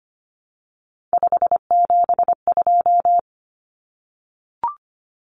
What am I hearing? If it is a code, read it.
Morse code, heard as a keyed tone, about 25 wpm, 700 Hz. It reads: 572
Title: Common Cut Numbers - 3-Digit 25wpm